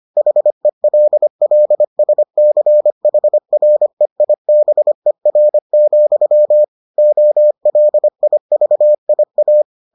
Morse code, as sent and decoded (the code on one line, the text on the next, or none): .... .-.. .-.. ... -.-. .... .-. .. -... .-. --..-- / --- .-.. .. ...- .. .-
HELLSCHREIBER, OLIVIA